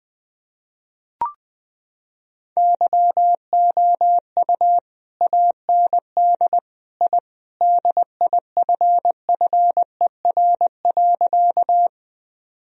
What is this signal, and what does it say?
Morse code, heard as a keyed tone, about 21 wpm, 700 Hz. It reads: YOU AND I DIFFER.
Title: Sentences from Top 1000 Words - Review Part 1 - 20wpm